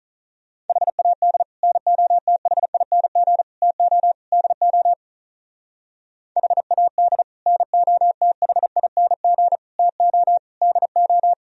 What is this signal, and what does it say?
Morse code, heard as a keyed tone, about 44 wpm, 700 Hz. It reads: HAD NOTHING TO DO HAD NOTHING TO DO
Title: Speed-Racing - Sets of 4 Words 35wpm